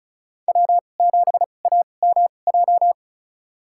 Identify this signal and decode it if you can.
Morse code, heard as a keyed tone, about 36 wpm, spaced as for 25 wpm, 700 Hz. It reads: W7AMJ
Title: USA Call Signs - 5 Characters - 2x Character Spacing - 35wpm